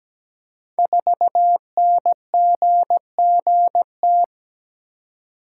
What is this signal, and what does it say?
Morse code, heard as a keyed tone, about 17 wpm, 700 Hz. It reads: 4NGGT
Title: Code Groups - 5 Letter:Numbers - 17wpm